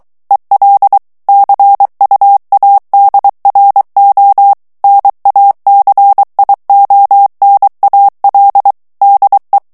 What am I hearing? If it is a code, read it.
Morse code, heard as a keyed tone, about 23 wpm, 800 Hz. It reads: EL CUADRO NACIONAL DE
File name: morse mp3 / ESPAGNOL TEXTE EN CLAIR 2 VITESSE 22 MOTS MINUTES